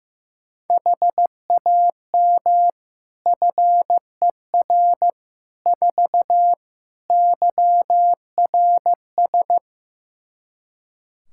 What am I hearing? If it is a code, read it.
Morse code, heard as a keyed tone, about 15 wpm, 700 Hz. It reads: HAM FER 4 YRS
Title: Practice QSO - Exchange 3 - More Info - 15wpm